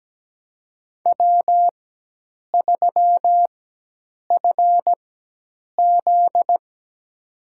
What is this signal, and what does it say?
Morse code, heard as a keyed tone, about 17 wpm, spaced as for 8 wpm, 700 Hz. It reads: W3FZ
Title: USA Call Signs - 4 Characters - 4x Character Spacing - 17wpm